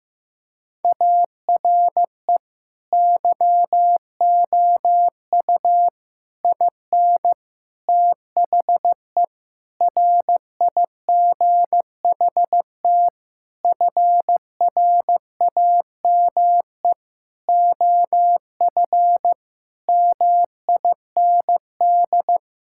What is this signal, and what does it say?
Morse code, heard as a keyed tone, about 15 wpm, 700 Hz. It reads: ARE YOU IN THE RIGHT FRAME OF MIND
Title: Sentences from Top 1000 Words - Review Part 6 - 15wpm